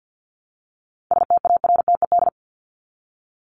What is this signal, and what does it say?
Morse code, heard as a keyed tone, about 50 wpm, 700 Hz. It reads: STARTED
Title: ED Suffix - Top 100 Words with Base Words 50wpm